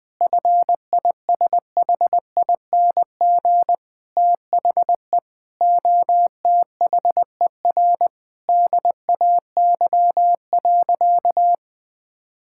Morse code, heard as a keyed tone, about 21 wpm, 700 Hz. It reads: FISHING THE OTHER DAY.
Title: Sentences from Top 1000 Words - Review Part 5 - 20wpm